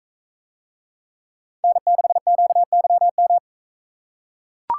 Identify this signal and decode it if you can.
Morse code, heard as a keyed tone, about 41 wpm, 700 Hz. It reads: N6QYM